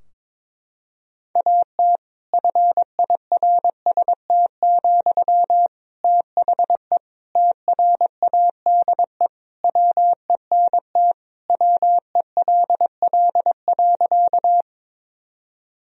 Morse code, heard as a keyed tone, about 21 wpm, 700 Hz. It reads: AT FIRST, THE TRADE WENT WELL.
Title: Sentences from Top 1000 Words - Review Part 4 - 22wpm